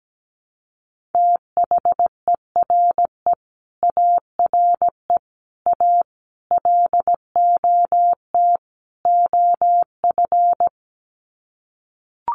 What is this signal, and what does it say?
Morse code, heard as a keyed tone, about 17 wpm, 700 Hz. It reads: THERE ARE A LOT OF
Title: Sets of 5 Words 17wpm